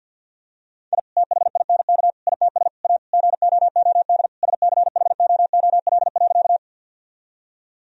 Morse code, heard as a keyed tone, about 48 wpm, 700 Hz. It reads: I THINK ITS A GOOD SCHOOL.
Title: Sentences from Top 1000 Words - Review Part 1 - 50wpm